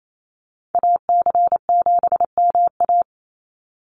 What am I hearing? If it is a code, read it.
Morse code, heard as a keyed tone, about 28 wpm, 700 Hz. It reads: AC7MA
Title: NA SOTA Activator Callsigns 28wpm